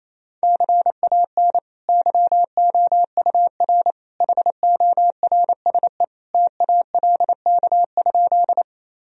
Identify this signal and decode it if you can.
Morse code, heard as a keyed tone, about 28 wpm, 700 Hz. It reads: CAN YOUR HORSE TALK?
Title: Sentences from Top 1000 Words - Review Part 1 - 28wpm